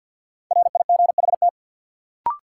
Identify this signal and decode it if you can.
Morse code, heard as a keyed tone, about 53 wpm, 700 Hz. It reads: RIGHT
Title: Sentences from Top 200 Words - New Word Drill - 50wpm